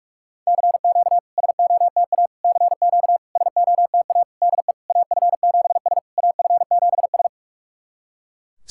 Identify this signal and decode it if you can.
Morse code, heard as a keyed tone, about 44 wpm, 690 Hz. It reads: CQ SOTA CQ SOTA DE AF7S AF7S
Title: SOTA QSO Practice 45wpm